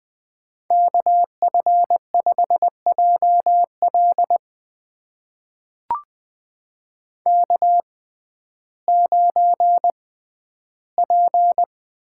Morse code, heard as a keyed tone, about 20 wpm, spaced as for 12 wpm, 700 Hz. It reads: KF5JL K9P
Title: USA Call Signs - 5 Characters - 6x Character Spacing - 20wpm